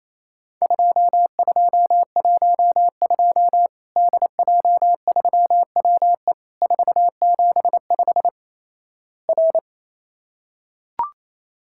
Morse code, heard as a keyed tone, about 28 wpm, 700 Hz. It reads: 2212 DJ3WE 475 R
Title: Worked All Europe DX Contest - QTC Exchange - 28wpm